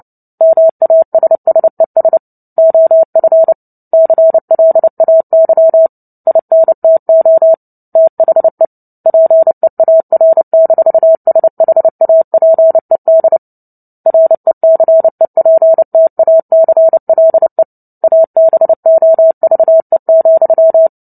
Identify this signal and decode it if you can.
Morse code, heard as a keyed tone, about 30 wpm, 650 Hz. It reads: MASSES OF CLAY INTO THE PEAR-SHAPED RECEPTACLE ABOVE,
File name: Book Two / 03-Chapter 3 - 29 WPM